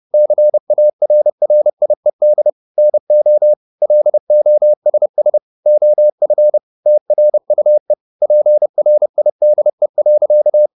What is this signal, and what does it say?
Morse code, heard as a keyed tone, about 30 wpm, 600 Hz. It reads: CARRIED NO LOSS OF TRUE PRIDE.
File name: the old man and the sea / the old man and the sea 1 30wpm